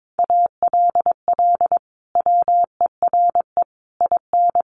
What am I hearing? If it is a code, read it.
Morse code, heard as a keyed tone, about 22 wpm, 700 Hz. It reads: ALL WERE IN
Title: Sentences from Top 1100 Words 22wpm